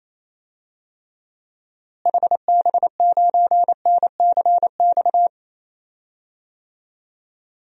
Morse code, heard as a keyed tone, about 28 wpm, 700 Hz. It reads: HB9NCX
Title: DXCC-100 Call Signs - 6 Characters - Repeated 2x - 28wpm